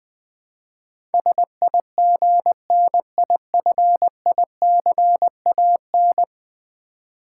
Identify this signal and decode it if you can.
Morse code, heard as a keyed tone, about 20 wpm, 700 Hz. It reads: SIGNIFICAN
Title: Build A Word - 5 Letters At A Time - 12-Letter Words - 20wpm